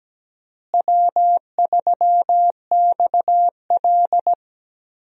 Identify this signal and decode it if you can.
Morse code, heard as a keyed tone, about 17 wpm, 700 Hz. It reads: W3XL
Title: USA Call Signs - 4 Characters - Repeated 3x - 17wpm